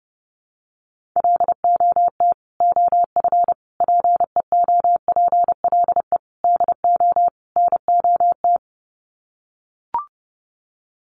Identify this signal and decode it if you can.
Morse code, heard as a keyed tone, about 30 wpm, 700 Hz. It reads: LOT OF PEOPLE DO NOT
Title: Sets of 5 Words Encore 30wpm